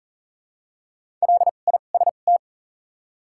Morse code, heard as a keyed tone, about 38 wpm, 700 Hz. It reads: L I S T
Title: 2x Character Spacing - Top 500 Words - 40wpm